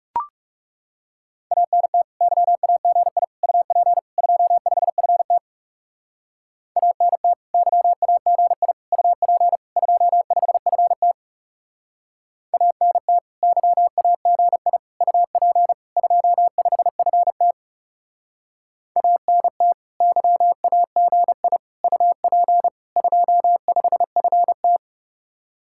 Morse code, heard as a keyed tone, about 38 wpm, 700 Hz. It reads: ANT YAGI UP 25FT ANT YAGI UP 25FT ANT YAGI UP 25FT ANT YAGI UP 25FT